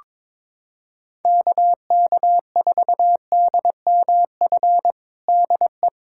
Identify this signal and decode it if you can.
Morse code, heard as a keyed tone, about 22 wpm, 700 Hz. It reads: KK4DMF DE
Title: USA Callsign DE Callsign 22wpm